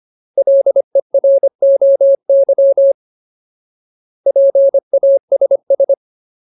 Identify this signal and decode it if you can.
Morse code, heard as a keyed tone, about 25 wpm, 550 Hz. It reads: LEROY PASS
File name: Ultimate Word List / 25WPM Ultimate